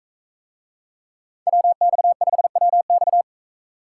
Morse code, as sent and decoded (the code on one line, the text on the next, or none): .-- -..- ..... .-- -..-
WX5WX